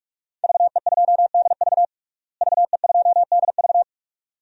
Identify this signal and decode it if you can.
Morse code, heard as a keyed tone, about 46 wpm, 700 Hz. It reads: VE2DV VE2DV
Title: DXCC-100 Call Signs - 5 Characters - Repeated 2x - 45wpm